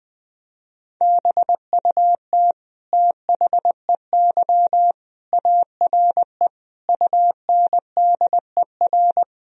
Morse code, heard as a keyed tone, about 21 wpm, 700 Hz. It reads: BUT THEY ARE UNDER
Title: Sentences from Top 1000 Words - Review Part 4 - 20wpm